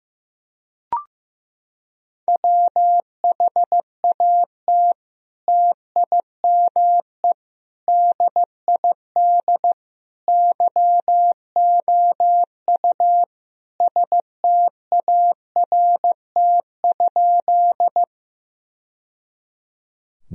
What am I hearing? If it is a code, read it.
Morse code, heard as a keyed tone, about 15 wpm, 700 Hz. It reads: WHAT TIME DID YOU START?
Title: Sentences from Top 1000 Words - Review Part 1 - 15wpm